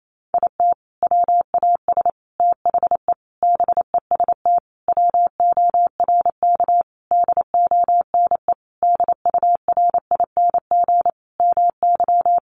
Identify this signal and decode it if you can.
Morse code, heard as a keyed tone, about 28 wpm, 700 Hz. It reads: IT WAS THE BEST WORK DONE DURING MY
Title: Sentences from Top 1300 Words 28wpm